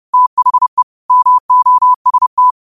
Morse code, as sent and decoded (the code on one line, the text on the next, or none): - .... / -- --- ... -
THE MOST